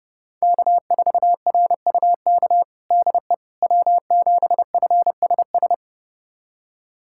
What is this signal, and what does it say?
Morse code, heard as a keyed tone, about 32 wpm, 700 Hz. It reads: K4RUK DE W7FSS
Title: USA Callsign DE Callsign - 5 Characters - 30wpm